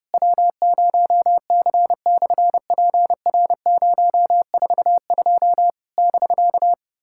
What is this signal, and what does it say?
Morse code, heard as a keyed tone, about 30 wpm, 700 Hz. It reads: W0C/PR042 <BK>
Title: SOTA QSO Practice 30wpm